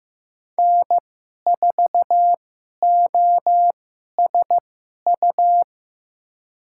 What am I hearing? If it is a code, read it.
Morse code, heard as a keyed tone, about 15 wpm, spaced as for 11 wpm, 700 Hz. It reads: N4OSU
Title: USA Call Signs - 5 Characters - 2x Character Spacing - 15wpm